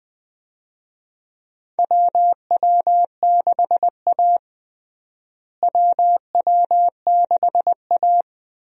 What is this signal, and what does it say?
Morse code, heard as a keyed tone, about 21 wpm, 700 Hz. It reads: WW6A WW6A
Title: USA Call Signs - 4 Characters - Repeated 2x - 20wpm